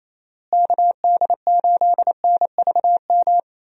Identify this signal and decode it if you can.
Morse code, heard as a keyed tone, about 28 wpm, 700 Hz. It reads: KD8NVM
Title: USA Call Signs - 6 Characters - Repeated 2x - 28wpm